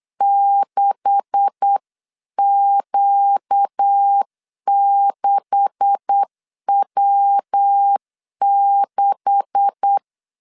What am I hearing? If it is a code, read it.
Morse code, heard as a keyed tone, about 8 wpm, 800 Hz. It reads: 6Q6W6
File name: LEZIONI / 9 286 8wpm 3425